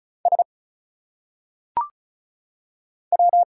Morse code, heard as a keyed tone, about 35 wpm, 700 Hz. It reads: S W
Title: Single Letters - Phonetic Pronunciation 35wpm